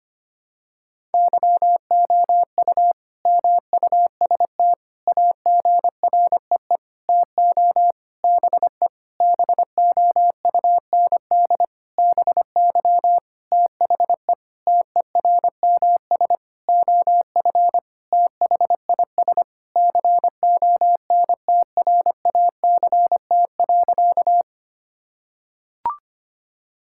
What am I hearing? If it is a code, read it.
Morse code, heard as a keyed tone, about 25 wpm, 700 Hz. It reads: YOU MUST AGREE TO BE BOUND BY THE TERMS OF THIS CONTRACT.
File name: practice / 25wpm / Sentences with 9-Word Phrases - 25wpm